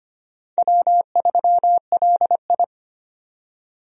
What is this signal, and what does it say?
Morse code, heard as a keyed tone, about 25 wpm, 700 Hz. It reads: W3LI
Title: USA Call Signs - 4 Characters - 25wpm